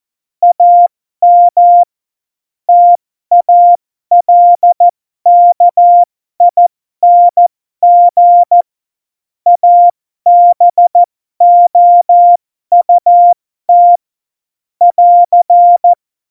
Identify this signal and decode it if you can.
Morse code, heard as a keyed tone, about 14 wpm, spaced as for 12 wpm, 700 Hz. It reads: AM TALKING ABOUT <AR>
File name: Morsekurs / Witze